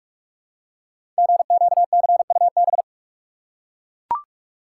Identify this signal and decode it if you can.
Morse code, heard as a keyed tone, about 41 wpm, 700 Hz. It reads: GQCUB
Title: Code Groups - 5 Letter:Numbers - 45wpm